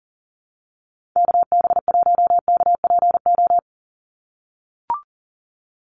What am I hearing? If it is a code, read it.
Morse code, heard as a keyed tone, about 40 wpm, 700 Hz. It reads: KB1KPO
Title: USA Call Signs 40wpm